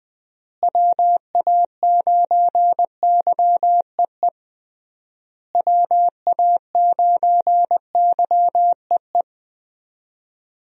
Morse code, heard as a keyed tone, about 20 wpm, 700 Hz. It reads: WA9YEE WA9YEE
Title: USA Call Signs - 6 Characters - Repeated 3x - 20wpm